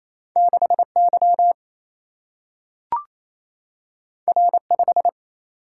Morse code, heard as a keyed tone, about 28 wpm, 700 Hz. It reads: 6Y R5
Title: Single-Letter Single-Number 28wpm